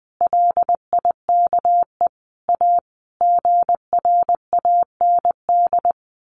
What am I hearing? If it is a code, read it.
Morse code, heard as a keyed tone, about 20 wpm, 700 Hz. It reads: LIKE A GRAND
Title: A Sound Of Thunder 20wpm